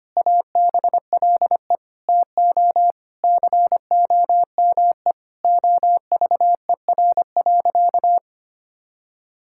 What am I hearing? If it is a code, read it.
Morse code, heard as a keyed tone, about 25 wpm, 700 Hz. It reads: ABLE TO COME OVER.
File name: practice / 25wpm / Sentences with 4-Word Phrases - 25wpm